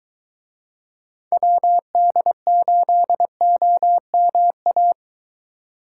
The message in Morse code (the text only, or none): WD8OMA